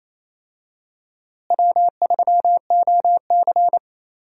Morse code, heard as a keyed tone, about 28 wpm, 700 Hz. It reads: W3OC
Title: USA Call Signs - 4 Characters - Repeated 2x - 28wpm